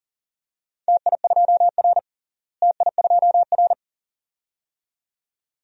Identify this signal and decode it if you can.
Morse code, heard as a keyed tone, about 37 wpm, 700 Hz. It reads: TI2R TI2R
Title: DXCC-100 Call Signs - 4 Characters - Repeated 2x - 40wpm